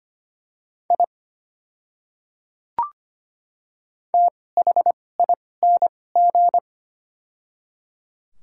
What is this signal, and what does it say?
Morse code, heard as a keyed tone, about 25 wpm, spaced as for 18 wpm, 700 Hz. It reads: I THING
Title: 2x Character Spacing - Top 100 Words - 25wpm